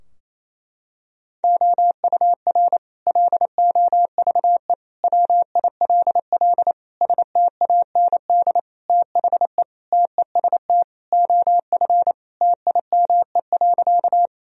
Morse code, heard as a keyed tone, about 28 wpm, 700 Hz. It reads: OUR LOVE WILL STAND THE TEST OF TIME.
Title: Sentences from Top 1000 Words - Review Part 3 - 28wpm